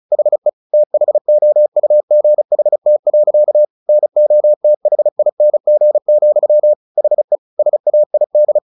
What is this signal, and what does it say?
Morse code, heard as a keyed tone, about 34 wpm, 600 Hz. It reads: HE THOUGHT. NOTHING, HE SAID
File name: the old man and the sea 2 35wpm